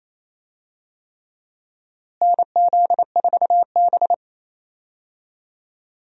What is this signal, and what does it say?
Morse code, heard as a keyed tone, about 28 wpm, 700 Hz. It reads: NZ4B
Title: USA Call Signs - 4 Characters - Repeated 3x - 28wpm